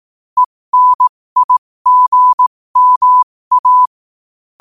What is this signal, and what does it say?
Morse code, heard as a keyed tone, about 18 wpm, spaced as for 15 wpm, 1000 Hz. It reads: ENIGMA
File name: qotdfiles / 15wpm